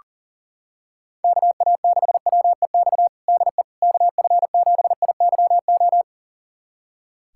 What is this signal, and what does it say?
Morse code, heard as a keyed tone, about 37 wpm, 700 Hz. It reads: KA6WEX DE KF7IYO
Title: USA Callsign DE Callsign - 6 Characters - 40wpm